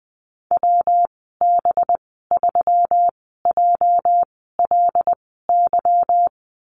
Morse code, heard as a keyed tone, about 20 wpm, spaced as for 14 wpm, 700 Hz. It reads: WB3JLY
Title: 245 Lesson 39g - Letter z - Call Signs 20wpm